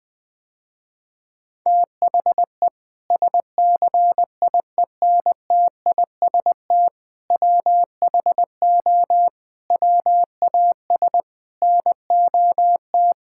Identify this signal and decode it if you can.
Morse code, heard as a keyed tone, about 20 wpm, 700 Hz. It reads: THE SCIENTIST WHO WAS NOT
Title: Sentences with 8-Word Phrases - Spoken Beginning - 20wpm